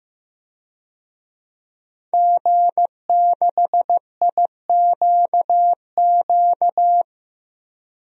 Morse code, heard as a keyed tone, about 16 wpm, 700 Hz. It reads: G6IQQ
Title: DXCC-100 Call Signs - 5 Characters - 15wpm